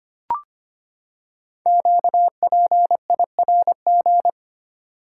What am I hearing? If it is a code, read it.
Morse code, heard as a keyed tone, about 25 wpm, 700 Hz. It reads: QPIRG
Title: Code Groups - 5 Letters - 25wpm